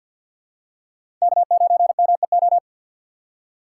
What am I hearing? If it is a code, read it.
Morse code, heard as a keyed tone, about 52 wpm, 700 Hz. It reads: K9MEO
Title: USA Call Signs - 5 Characters - 3x Character Spacing - 50wpm